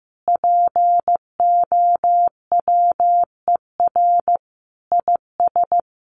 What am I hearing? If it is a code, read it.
Morse code, heard as a keyed tone, about 15 wpm, 700 Hz. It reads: POWER IS
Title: Sentences from Top 1500 Words 15wpm